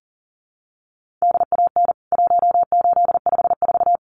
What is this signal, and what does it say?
Morse code, heard as a keyed tone, about 40 wpm, 700 Hz. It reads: DAN 1854
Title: CWT Exchange 40wpm